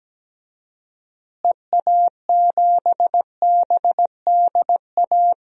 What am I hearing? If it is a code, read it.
Morse code, heard as a keyed tone, about 17 wpm, 700 Hz. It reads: EA7BDA
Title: DXCC-100 Call Signs - 6 Characters - Repeated 2x - 17wpm